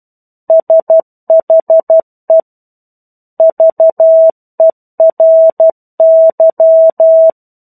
Morse code, heard as a keyed tone, about 12 wpm, 650 Hz. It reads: SHE VERY
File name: 01-Chapter-01 12 WPM